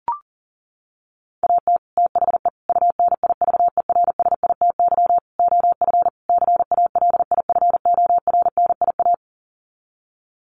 Morse code, heard as a keyed tone, about 37 wpm, 700 Hz. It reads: AT THE UNIVERSITY OF CALIFORNIA
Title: Sets of 5 Words Encore2 40wpm